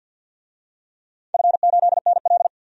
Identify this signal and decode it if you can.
Morse code, heard as a keyed tone, about 50 wpm, 700 Hz. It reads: F8NL
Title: DXCC-100 Call Signs - 4 Characters - 50wpm